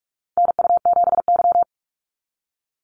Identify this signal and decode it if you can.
Morse code, heard as a keyed tone, about 44 wpm, 700 Hz. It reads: NU7Y
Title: NA SOTA Activator Callsigns 45wpm